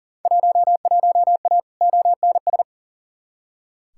Morse code, heard as a keyed tone, about 37 wpm, 700 Hz. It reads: 11A ONS
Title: Field Day Exchanges Encore - 40wpm